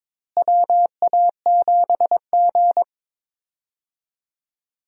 Morse code, heard as a keyed tone, about 22 wpm, 700 Hz. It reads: WA7G
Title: USA Call Signs - 4 Characters - Repeated 2x - 22wpm